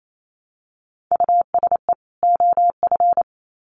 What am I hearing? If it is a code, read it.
Morse code, heard as a keyed tone, about 28 wpm, 700 Hz. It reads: USE OF
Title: Sets of 2 Words 28wpm